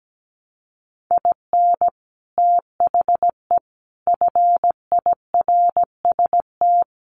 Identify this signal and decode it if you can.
Morse code, heard as a keyed tone, about 17 wpm, 700 Hz. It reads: IN THE FIRST
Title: Sets of 5 Words Encore2 17wpm